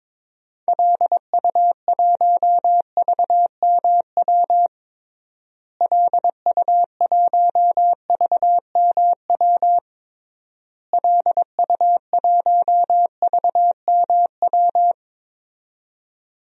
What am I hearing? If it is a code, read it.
Morse code, heard as a keyed tone, about 22 wpm, 700 Hz. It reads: LU1VMW LU1VMW LU1VMW
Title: DXCC-100 Call Signs - 6 Characters - Repeated 3x - 22wpm